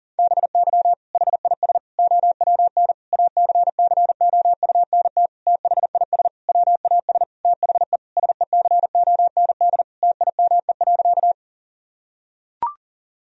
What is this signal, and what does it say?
Morse code, heard as a keyed tone, about 37 wpm, 700 Hz. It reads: BY HIS OWN ACCOUNT THIS WAS THE SECOND TIME.
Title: Sentences from Top 1000 Words - Review Part 1 - 40wpm